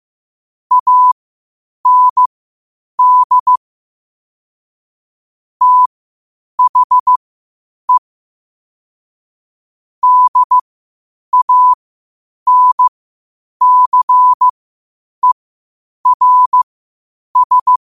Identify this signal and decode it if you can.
Morse code, heard as a keyed tone, about 15 wpm, spaced as for 8 wpm, 1000 Hz. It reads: AND THE DANCERS
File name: qotdfiles / 8wpm